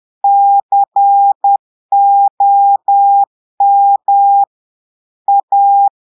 Morse code, heard as a keyed tone, about 10 wpm, 800 Hz. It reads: COM A
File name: Portugais - texte en clair 1 - vitesse - 10 mots minute